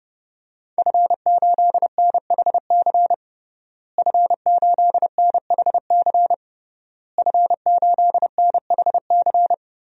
Morse code, heard as a keyed tone, about 30 wpm, 700 Hz. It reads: F8NHC F8NHC F8NHC
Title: DXCC-100 Call Signs - 5 Characters - Repeated 3x - 30wpm